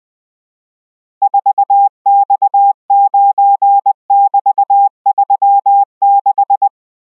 Morse code, heard as a keyed tone, about 20 wpm, 800 Hz. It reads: 4X9<BT>36